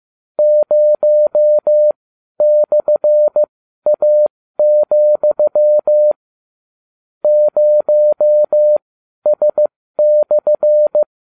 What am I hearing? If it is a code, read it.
Morse code, heard as a keyed tone, about 15 wpm, spaced as for 12 wpm, 600 Hz. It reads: 0/A, 0S/